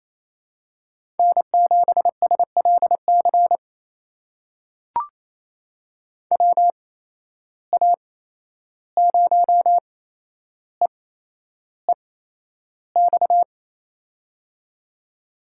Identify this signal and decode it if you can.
Morse code, heard as a keyed tone, about 28 wpm, spaced as for 11 wpm, 700 Hz. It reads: N7SLC WA0EEX
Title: USA Call Signs - 8x Character Spacing - 28wpm